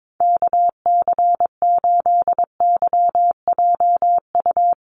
Morse code, heard as a keyed tone, about 22 wpm, 700 Hz. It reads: KC8YJU
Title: USA Call Signs Repeated 3x 22wpm